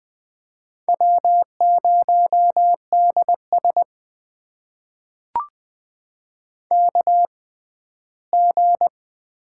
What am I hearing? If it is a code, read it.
Morse code, heard as a keyed tone, about 21 wpm, spaced as for 14 wpm, 700 Hz. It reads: W0DS KG
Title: USA Call Signs - 6x Character Spacing - 20wpm